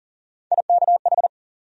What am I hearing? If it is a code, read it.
Morse code, heard as a keyed tone, about 40 wpm, 700 Hz. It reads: IKH
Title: USA Call Sign Postfixes - 3 Letters 40wpm